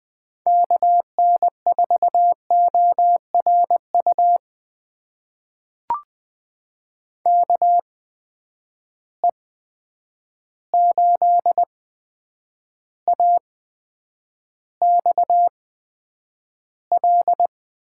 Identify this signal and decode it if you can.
Morse code, heard as a keyed tone, about 20 wpm, spaced as for 9 wpm, 700 Hz. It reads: KN4ORU KE8AXL
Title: USA Call Signs - 6 Characters - 8x Character Spacing - 20wpm